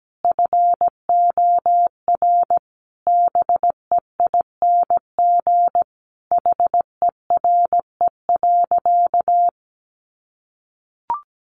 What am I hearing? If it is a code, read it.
Morse code, heard as a keyed tone, about 17 wpm, 700 Hz. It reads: FOR BEING HERE.
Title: Sentences from Top 700 Words 17wpm